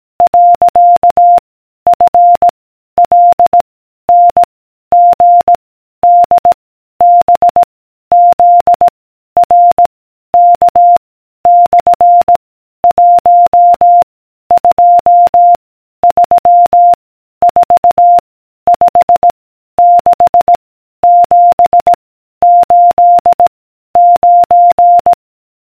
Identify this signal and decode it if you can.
Morse code, heard as a keyed tone, about 18 wpm, 700 Hz. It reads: . F L N G D B Z R K / 1 2 3 4 5 6 7 8 9